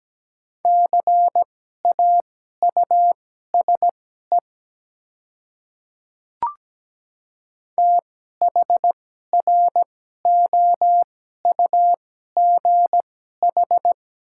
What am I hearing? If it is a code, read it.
Morse code, heard as a keyed tone, about 17 wpm, spaced as for 12 wpm, 700 Hz. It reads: CAUSE THROUGH
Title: 2x Character Spacing - Top 300 Words - 17wpm